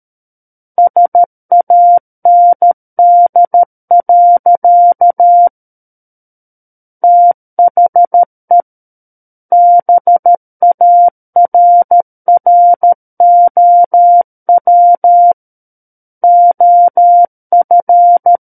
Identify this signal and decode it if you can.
Morse code, heard as a keyed tone, about 13 wpm, 700 Hz. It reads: SAND. THE BARROW OF
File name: Book One / 04-Chapter 4 - 13 WPM